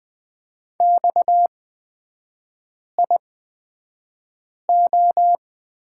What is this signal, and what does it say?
Morse code, heard as a keyed tone, about 20 wpm, spaced as for 5 wpm, 700 Hz. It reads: XIO
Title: 243 Lesson 39e - Letter z - Three Characters 20wpm